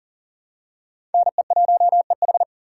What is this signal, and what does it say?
Morse code, heard as a keyed tone, about 40 wpm, 700 Hz. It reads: NE1EH
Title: USA Call Signs - 2x Character Spacing - 40wpm